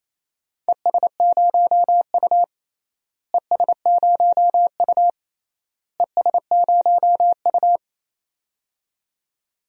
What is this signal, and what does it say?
Morse code, heard as a keyed tone, about 28 wpm, 700 Hz. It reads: ES0U ES0U ES0U
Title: DXCC-100 Call Signs - 4 Characters - Repeated 3x - 28wpm